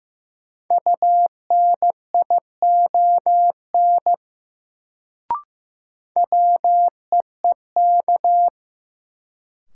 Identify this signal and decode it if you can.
Morse code, heard as a keyed tone, about 15 wpm, 700 Hz. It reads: UNION WEEK
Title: Sets of 5 Words Encore - New Word Drill - 15wpm